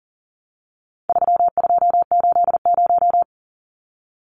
Morse code, heard as a keyed tone, about 40 wpm, 700 Hz. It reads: 3280
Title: Numbers - 4-Digit 40wpm